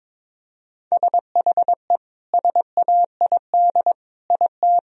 Morse code, heard as a keyed tone, about 21 wpm, 700 Hz. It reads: SHE SAID IT
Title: Sentences from Top 1000 Words - Review Part 1 - 22wpm